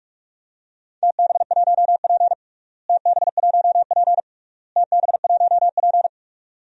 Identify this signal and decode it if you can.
Morse code, heard as a keyed tone, about 42 wpm, 700 Hz. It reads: TB1P TB1P TB1P
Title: DXCC-100 Call Signs - 4 Characters - Repeated 3x - 45wpm